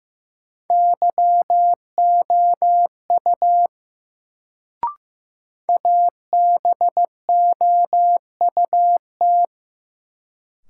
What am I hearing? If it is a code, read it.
Morse code, heard as a keyed tone, about 15 wpm, 700 Hz. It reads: YOU ABOUT
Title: Sentences from Top 100 Words - New Word Drill - 15wpm